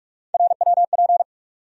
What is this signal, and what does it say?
Morse code, heard as a keyed tone, about 46 wpm, 700 Hz. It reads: RWP
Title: USA Call Sign Postfixes - 3 Letters 45wpm